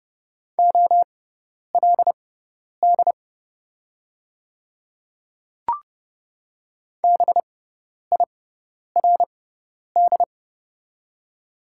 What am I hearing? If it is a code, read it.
Morse code, heard as a keyed tone, about 30 wpm, spaced as for 10 wpm, 700 Hz. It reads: OLD BIRD
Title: 6x Character Spacing - Top 500 Words - 30wpm